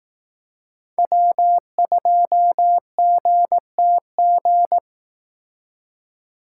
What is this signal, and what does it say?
Morse code, heard as a keyed tone, about 18 wpm, 700 Hz. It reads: W2GTG
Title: Speed Racing - USA 1x3 Call Signs 15wpm